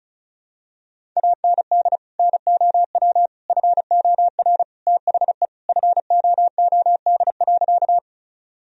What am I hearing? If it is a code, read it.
Morse code, heard as a keyed tone, about 35 wpm, 700 Hz. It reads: AND NOW FOR THE FOOD.
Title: Sentences from Top 1000 Words - Review Part 1 - 35wpm